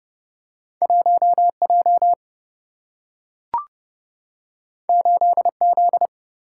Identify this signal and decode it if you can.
Morse code, heard as a keyed tone, about 30 wpm, 700 Hz. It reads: 1 J 8 Z
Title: Single-Letter Single-Number 30wpm